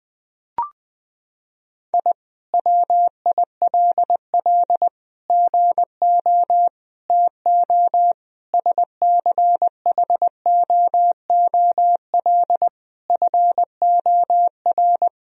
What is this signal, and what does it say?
Morse code, heard as a keyed tone, about 20 wpm, 700 Hz. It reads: I WILL GO TO SCHOOL FOR
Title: Sentences from Top 1000 Words - Review Part 4 - 20wpm